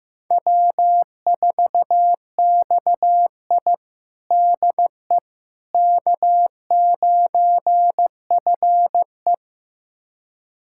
Morse code, heard as a keyed tone, about 15 wpm, 700 Hz. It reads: W4XI DE K9FE
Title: USA Callsign DE Callsign - 4 Characters - 15wpm